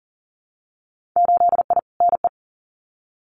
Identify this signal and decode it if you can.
Morse code, heard as a keyed tone, about 40 wpm, 700 Hz. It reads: 8I NE
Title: Winter Field Day Exchanges Encore 40wpm